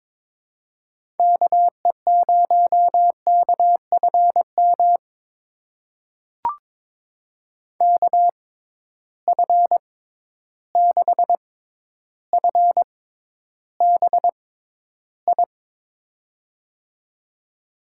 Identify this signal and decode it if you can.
Morse code, heard as a keyed tone, about 22 wpm, spaced as for 11 wpm, 700 Hz. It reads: KE0KFM KF6FBI
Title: USA Call Signs - 6 Characters - 6x Character Spacing - 22wpm